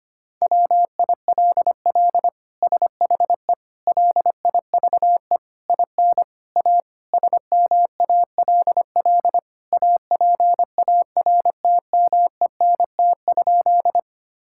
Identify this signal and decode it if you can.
Morse code, heard as a keyed tone, about 25 wpm, 700 Hz. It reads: WILL SHE LIVE IN A SMALL APARTMENT?
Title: Sentences from Top 1000 Words - Review Part 6 - 25wpm